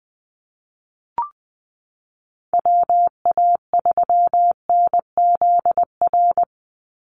Morse code, heard as a keyed tone, about 19 wpm, 700 Hz. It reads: WA3NZR
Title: Call Signs Repeated 3x - 20wpm